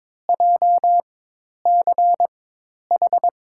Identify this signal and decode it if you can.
Morse code, heard as a keyed tone, about 22 wpm, spaced as for 10 wpm, 700 Hz. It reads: JCH
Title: USA Call Signs - 6 Characters - 4x Character Spacing - 22wpm